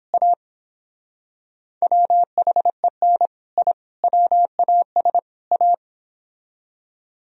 Morse code, heard as a keyed tone, about 31 wpm, 700 Hz. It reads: A WHEN I WAS A